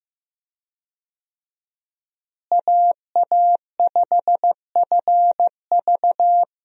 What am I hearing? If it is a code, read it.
Morse code, heard as a keyed tone, about 15 wpm, 700 Hz. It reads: AA5FV
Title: USA Call Signs - 5 Characters - Repeated 2x - 15wpm